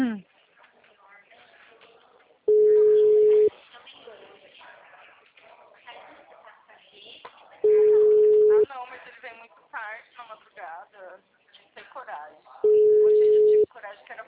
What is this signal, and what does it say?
Ringback tone (Europe, 425 Hz), with speech.